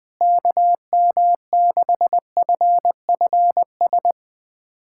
Morse code, heard as a keyed tone, about 20 wpm, 700 Hz. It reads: KM6FFS
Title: USA Callsign DE Callsign 20wpm